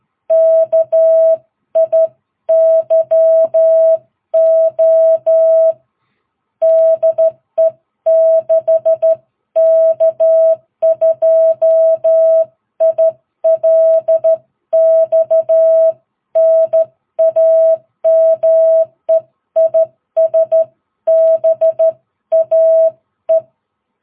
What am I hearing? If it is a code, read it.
Morse code, heard as a keyed tone, about 12 wpm, 650 Hz. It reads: KIYO DE 6K2ILX NAME IS BAE
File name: jh8cbh / 2022cw